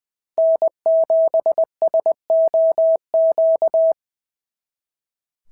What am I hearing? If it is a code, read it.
Morse code, heard as a keyed tone, about 19 wpm, 650 Hz. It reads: N7SOQ
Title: SOTA QSO Practice 20wpm